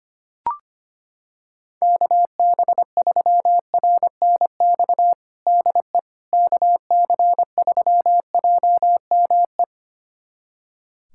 Morse code, heard as a keyed tone, about 25 wpm, 700 Hz. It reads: KB3RNX DE KC3JME
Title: USA Callsign DE Callsign 25wpm